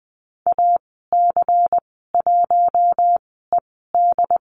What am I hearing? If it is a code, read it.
Morse code, heard as a keyed tone, about 19 wpm, spaced as for 14 wpm, 700 Hz. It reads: AC1ED
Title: 179 Lesson 29g - Slash - Call Signs 20wpm